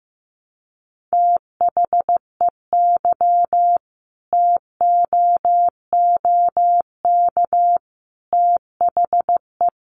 Morse code, heard as a keyed tone, about 15 wpm, 700 Hz. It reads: THEY TOOK THE
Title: Sentences from Top 1200 Words 15wpm